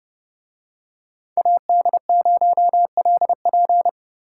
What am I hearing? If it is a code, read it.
Morse code, heard as a keyed tone, about 28 wpm, 700 Hz. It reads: AD0LP